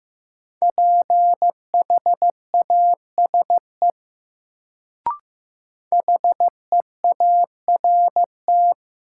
Morse code, heard as a keyed tone, about 15 wpm, 700 Hz. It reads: PHASE HEART
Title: Sentences from Top 1200 Words - New Word Drill - 15wpm